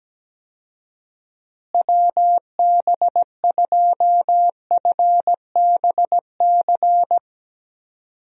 Morse code, heard as a keyed tone, about 17 wpm, 700 Hz. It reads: WB2FBC
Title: USA Call Signs - 6 Characters - Repeated 3x - 17wpm